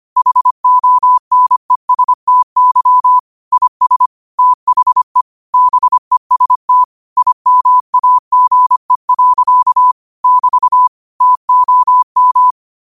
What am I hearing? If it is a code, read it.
Morse code, heard as a keyed tone, about 25 wpm, 1000 Hz. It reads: HONESTY IS THE BEST IMAGE. <BT> TOM
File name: qotdfiles / 25wpm